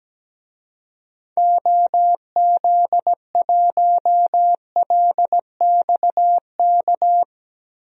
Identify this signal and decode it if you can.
Morse code, heard as a keyed tone, about 17 wpm, 700 Hz. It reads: OZ1LXK
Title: DXCC-100 Call Signs - 6 Characters - Repeated 2x - 17wpm